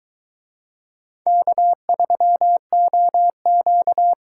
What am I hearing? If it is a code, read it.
Morse code, heard as a keyed tone, about 22 wpm, 700 Hz. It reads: K3OQ